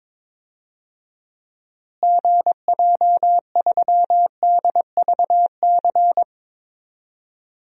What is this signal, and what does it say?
Morse code, heard as a keyed tone, about 22 wpm, 700 Hz. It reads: GJ3DVC
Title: DXCC-100 Call Signs - 6 Characters - Repeated 2x - 22wpm